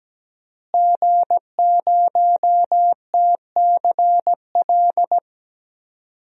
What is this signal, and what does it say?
Morse code, heard as a keyed tone, about 17 wpm, 700 Hz. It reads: G0TCL
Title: DXCC-100 Call Signs - 5 Characters - Repeated 2x - 17wpm